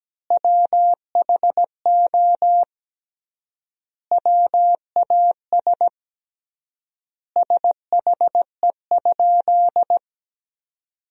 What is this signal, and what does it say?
Morse code, heard as a keyed tone, about 17 wpm, 700 Hz. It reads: WHO WAS SHE?
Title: 3x Word-Spacing - Sentences from Top 100 Words - 17wpm